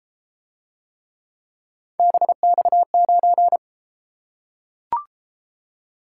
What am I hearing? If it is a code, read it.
Morse code, heard as a keyed tone, about 30 wpm, 700 Hz. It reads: BX9